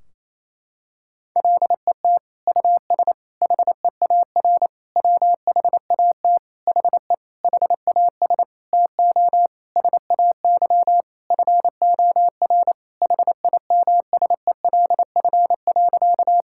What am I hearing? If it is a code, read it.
Morse code, heard as a keyed tone, about 27 wpm, 700 Hz. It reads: LET US HEAR WHAT HE HAS TO SAY FOR HIMSELF.
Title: Sentences from Top 1600 Words 28wpm